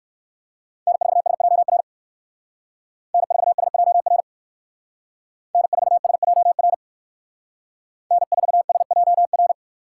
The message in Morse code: N4SJR N4SJR N4SJR N4SJR

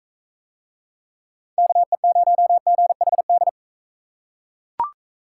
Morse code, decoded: KE0GHD

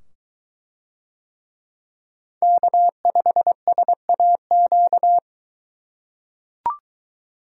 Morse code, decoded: K5SAQ

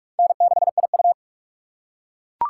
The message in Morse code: N6IU

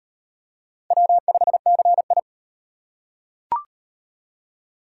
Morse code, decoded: W5CI

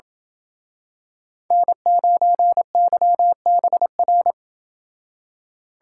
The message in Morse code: N9YBR

